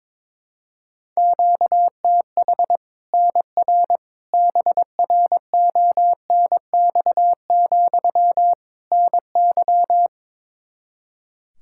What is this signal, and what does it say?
Morse code, heard as a keyed tone, about 21 wpm, 700 Hz. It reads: QTH NR BRONX, NY
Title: Practice QSO - Exchange 1 - Beginning - 22wpm